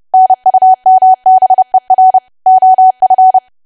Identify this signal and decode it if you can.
Morse code, heard as a keyed tone, about 30 wpm, 750 Hz. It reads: NUMBER OF
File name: cwa / W1AW-Level3 / 30WPM